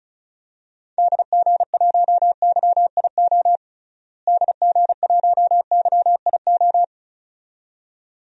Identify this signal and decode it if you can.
Morse code, heard as a keyed tone, about 36 wpm, 700 Hz. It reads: DG1YIO DG1YIO
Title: DXCC-100 Call Signs - 6 Characters - Repeated 2x - 35wpm